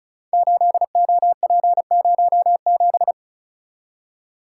Morse code, heard as a keyed tone, about 36 wpm, 700 Hz. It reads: 8OP07
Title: Code Groups - 5 Letter:Numbers - 35wpm